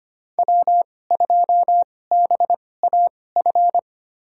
Morse code, heard as a keyed tone, about 25 wpm, spaced as for 18 wpm, 700 Hz. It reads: W2BAF
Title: USA Call Signs - 2x Character Spacing - 25wpm